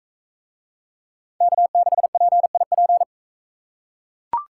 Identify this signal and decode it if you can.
Morse code, heard as a keyed tone, about 41 wpm, 700 Hz. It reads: K6PIP